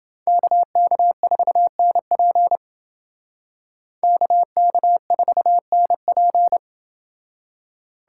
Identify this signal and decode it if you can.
Morse code, heard as a keyed tone, about 29 wpm, 700 Hz. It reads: KK4NP KK4NP